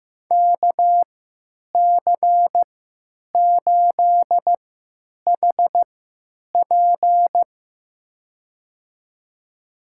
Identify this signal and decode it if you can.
Morse code, heard as a keyed tone, about 15 wpm, spaced as for 9 wpm, 700 Hz. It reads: KC8HP
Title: USA Call Signs - 5 Characters - 3x Character Spacing - 15wpm